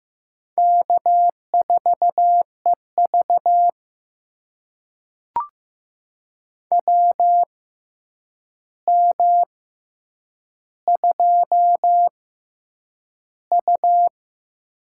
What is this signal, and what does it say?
Morse code, heard as a keyed tone, about 15 wpm, spaced as for 8 wpm, 700 Hz. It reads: K4EV WM2U
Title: USA Call Signs - 4 Characters - 6x Character Spacing - 15wpm